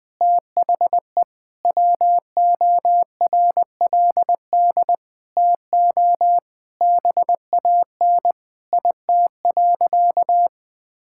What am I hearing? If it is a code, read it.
Morse code, heard as a keyed tone, about 20 wpm, 700 Hz. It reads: THE WORLD TO BAN IT.
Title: Sentences with 8-Word Phrases - 20wpm